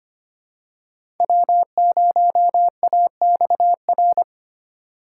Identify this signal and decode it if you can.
Morse code, heard as a keyed tone, about 25 wpm, 700 Hz. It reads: W0AXR
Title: USA Call Signs - 5 Characters - 4x Character Spacing - 25wpm